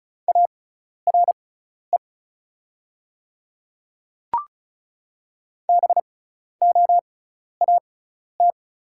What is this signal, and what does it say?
Morse code, heard as a keyed tone, about 36 wpm, spaced as for 12 wpm, 700 Hz. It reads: ARE BOAT
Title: 6x Character Spacing - Top 500 Words - 35wpm